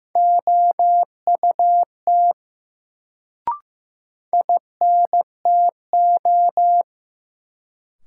Morse code, heard as a keyed tone, about 15 wpm, 700 Hz. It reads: OUT INTO
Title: Sets of 2 Words - New Word Drill - 15wpm